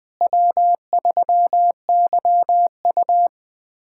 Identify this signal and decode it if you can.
Morse code, heard as a keyed tone, about 21 wpm, 700 Hz. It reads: W3YU
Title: USA Callsign DE Callsign - 4 Characters - 20wpm